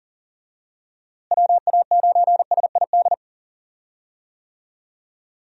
Morse code, heard as a keyed tone, about 37 wpm, 700 Hz. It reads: WA9SID
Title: USA Call Signs - 6 Characters - Repeated 2x - 40wpm